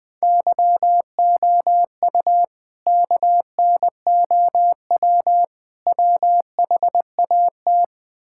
Morse code, heard as a keyed tone, about 20 wpm, 700 Hz. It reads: YOU KNOW WHAT
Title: Build A Phrase - 3-Word Phrases - 20wpm